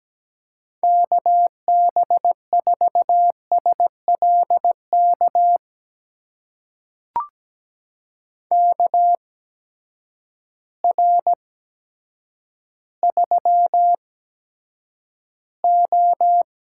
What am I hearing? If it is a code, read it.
Morse code, heard as a keyed tone, about 17 wpm, spaced as for 9 wpm, 700 Hz. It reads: KB4SLK KR3O
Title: USA Call Signs - 6 Characters - 8x Character Spacing - 17wpm